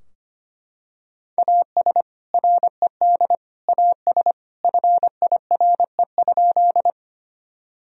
Morse code, heard as a keyed tone, about 25 wpm, 700 Hz. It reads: AS RED AS FIRE?
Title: Sentences from Top 1000 Words - Review Part 2 - 25wpm